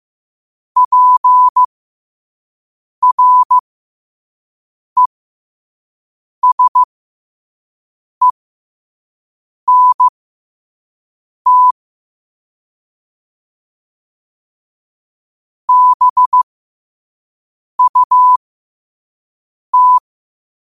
Morse code, heard as a keyed tone, about 15 wpm, spaced as for 5 wpm, 1000 Hz. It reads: PRESENT BUT
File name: qotdfiles / 5wpm